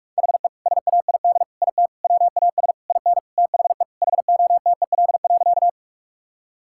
Morse code, heard as a keyed tone, about 46 wpm, 700 Hz. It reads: HE SAID IT WAS IN THE HOTEL.
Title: Sentences from Top 1600 Words 45wpm